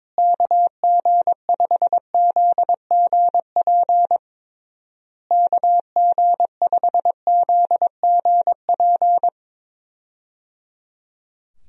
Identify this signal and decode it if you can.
Morse code, heard as a keyed tone, about 22 wpm, 700 Hz. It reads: KG5ZGP KG5ZGP
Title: USA Call Signs - 6 Characters - Repeated 3x - 22wpm